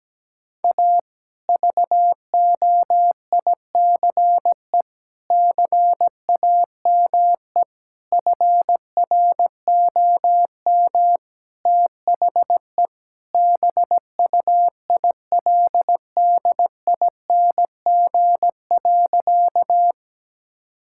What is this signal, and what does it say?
Morse code, heard as a keyed tone, about 17 wpm, 700 Hz. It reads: A VOICE CAME FROM THE BUILDING.
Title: Sentences from Top 1000 Words - Review Part 5 - 17wpm